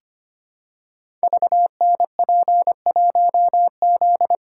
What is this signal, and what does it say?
Morse code, heard as a keyed tone, about 25 wpm, 700 Hz. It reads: VNP1Z
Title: Code Groups - 5 Letter:Numbers - 25wpm